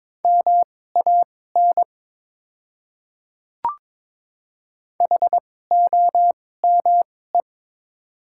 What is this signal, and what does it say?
Morse code, heard as a keyed tone, about 22 wpm, spaced as for 16 wpm, 700 Hz. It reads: MAN HOME
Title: 2x Character Spacing - Top 200 Words - 22wpm